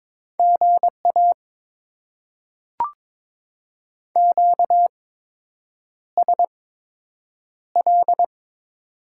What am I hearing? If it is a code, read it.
Morse code, heard as a keyed tone, about 22 wpm, spaced as for 8 wpm, 700 Hz. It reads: GA QSL
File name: Top 90 QSO Elements - 8x Character Spacing - 22wpm